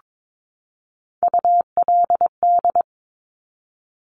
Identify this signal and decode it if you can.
Morse code, heard as a keyed tone, about 22 wpm, 700 Hz. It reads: ULD
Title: Top Three Letter Combinations in Words 22wpm